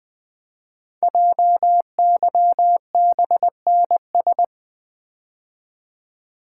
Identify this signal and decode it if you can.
Morse code, heard as a keyed tone, about 20 wpm, 700 Hz. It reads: JYBNS
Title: Code Groups - 5 Letters - 20wpm